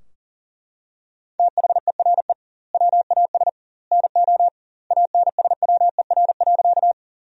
Morse code, heard as a keyed tone, about 40 wpm, 700 Hz. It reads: THERE WAS NO ANSWER.
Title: 2x Word-Spacing - Sentences from Top 300 Words - 40wpm